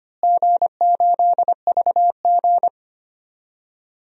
Morse code, heard as a keyed tone, about 25 wpm, 700 Hz. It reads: G8VG
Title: DXCC-100 Call Signs - 4 Characters - Repeated 3x - 25wpm